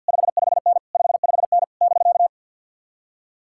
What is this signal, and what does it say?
Morse code, heard as a keyed tone, about 62 wpm, 700 Hz. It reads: 55N 55N <BK>
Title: Speed-Racing Common QSO Phrases - 50wpm